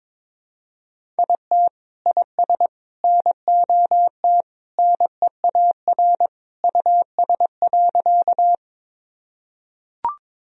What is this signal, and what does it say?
Morse code, heard as a keyed tone, about 22 wpm, 700 Hz. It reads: IT IS NOT NEAR US.
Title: Sentences from Top 1000 Words - Review Part 1 - 22wpm